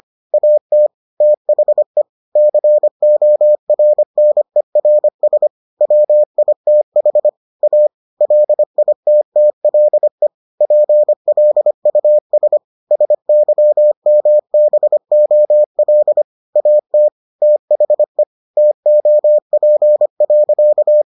Morse code, heard as a keyed tone, about 25 wpm, 600 Hz. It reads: AT THE CORNERS WITH A LITTLE PLUS SYMBOL AT THE TOP.